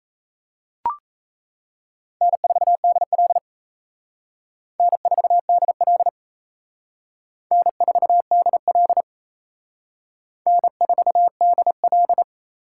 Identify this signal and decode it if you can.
Morse code, heard as a keyed tone, about 35 wpm, 700 Hz. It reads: N4DL N4DL N4DL N4DL
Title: Speed Racing - USA 1x2 Call Signs 28wpm